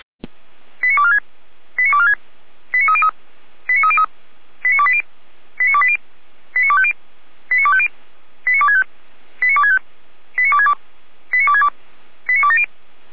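Five-tone selective call (ZVEI), heard as quick runs of five tones, tones of about 70 ms. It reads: 89246 89246 89303 89303 89280 89280 89250 89250 89265 89265 89262 89262 89270
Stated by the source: ZVEI five-tone call 89229 received via SDR.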